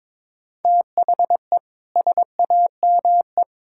Morse code, heard as a keyed tone, about 22 wpm, 700 Hz. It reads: THE SAME
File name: Sentences with 7-Word Phrases - 22wpm